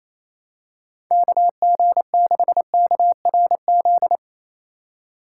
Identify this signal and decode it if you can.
Morse code, heard as a keyed tone, about 27 wpm, 700 Hz. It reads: KG6KRZ
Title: USA Call Signs - 6 Characters - 4x Character Spacing - 28wpm